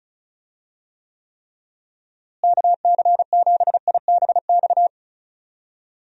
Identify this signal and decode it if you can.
Morse code, heard as a keyed tone, about 36 wpm, 700 Hz. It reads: KC7IBX
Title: USA Call Signs - 6 Characters - Repeated 3x - 35wpm